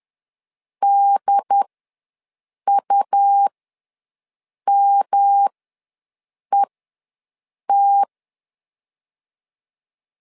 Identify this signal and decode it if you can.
Morse code, heard as a keyed tone, about 10 wpm, spaced as for 6 wpm, 800 Hz. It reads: DUMET
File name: LEZIONI / 3 MWRO / 3 MW ws 10wpm 2213